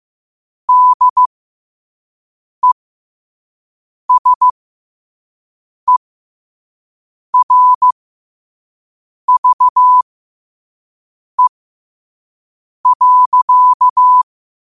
Morse code, heard as a keyed tone, about 15 wpm, spaced as for 5 wpm, 1000 Hz. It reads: DESERVE.
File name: qotdfiles / 5wpm